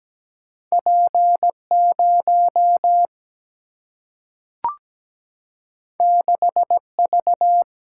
Morse code, heard as a keyed tone, about 17 wpm, 700 Hz. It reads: P 0 6 V
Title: Single-Letter Single-Number 17wpm